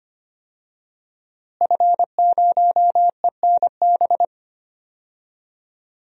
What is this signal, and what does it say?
Morse code, heard as a keyed tone, about 25 wpm, 700 Hz. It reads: F0ENB
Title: DXCC-100 Call Signs - 5 Characters - Repeated 3x - 25wpm